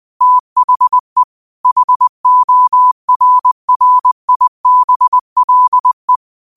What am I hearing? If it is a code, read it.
Morse code, heard as a keyed tone, about 20 wpm, 1000 Hz. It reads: THE HORRIBLE